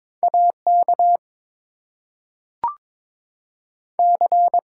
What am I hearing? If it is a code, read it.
Morse code, heard as a keyed tone, about 22 wpm, 700 Hz. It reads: AK C
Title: US States and Canadian Provinces 22wpm